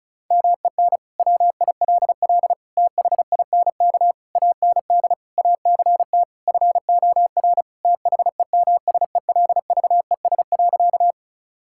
Morse code, heard as a keyed tone, about 36 wpm, 700 Hz. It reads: MEN WILL THINK AND ACT FOR THEMSELVES.
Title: Sentences from Top 1000 Words - Review Part 5 - 35wpm